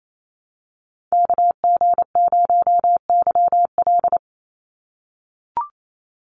Morse code, heard as a keyed tone, about 28 wpm, 700 Hz. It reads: KG0YL
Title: Call Signs - 28wpm